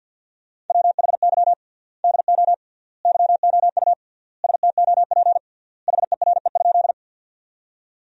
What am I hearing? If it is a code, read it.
Morse code, heard as a keyed tone, about 52 wpm, 700 Hz. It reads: WHY DO YOU STOP HERE?
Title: 3x Word-Spacing - Sentences from Top 300 Words - 50wpm